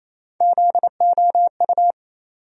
Morse code, heard as a keyed tone, about 28 wpm, 700 Hz. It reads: ZOU
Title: USA Call Sign Postfixes - 3 Letters 28wpm